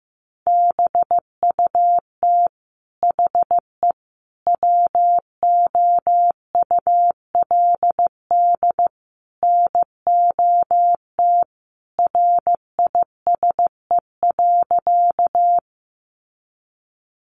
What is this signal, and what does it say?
Morse code, heard as a keyed tone, about 15 wpm, 700 Hz. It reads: BUT HE WOULD NOT RISE.
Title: Sentences from Top 900 Words 15wpm